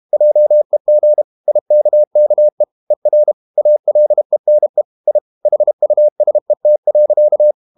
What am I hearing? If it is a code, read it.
Morse code, heard as a keyed tone, about 32 wpm, 600 Hz. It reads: JEG IKKE ER ALENE I HUSET.
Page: Ferie 32 wpm